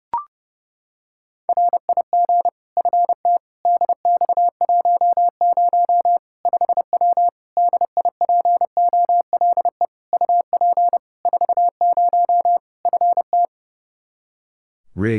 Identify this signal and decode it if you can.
Morse code, heard as a keyed tone, about 30 wpm, 700 Hz. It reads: RIG FT DX10 5W DIPOLE UP 40 FT
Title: Practice QSO - Exchange 2 - Info - 30wpm